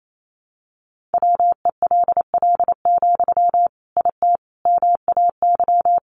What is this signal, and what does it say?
Morse code, heard as a keyed tone, about 28 wpm, 700 Hz. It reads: WELL, IT MAY
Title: Sentences from Top 1500 Words 28wpm